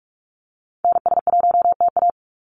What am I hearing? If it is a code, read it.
Morse code, heard as a keyed tone, about 44 wpm, 700 Hz. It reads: NS1TA
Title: NA SOTA Activator Callsigns 45wpm